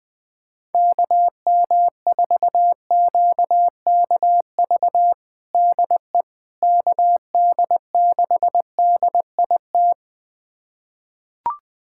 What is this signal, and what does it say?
Morse code, heard as a keyed tone, about 20 wpm, 700 Hz. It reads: KM4QKV DE KD6DIT
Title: USA Callsign DE Callsign - 6 Characters - 20wpm